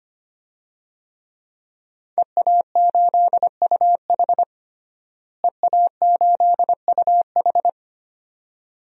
Morse code, heard as a keyed tone, about 25 wpm, 700 Hz. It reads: EA8UH EA8UH
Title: DXCC-100 Call Signs - 5 Characters - Repeated 2x - 25wpm